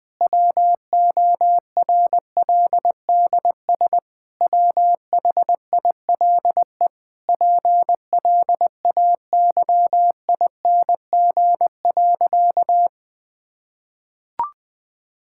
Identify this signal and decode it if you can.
Morse code, heard as a keyed tone, about 21 wpm, 700 Hz. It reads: WORLDS WHILE PLAYING.
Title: Difficult Sentences with Hints - 20wpm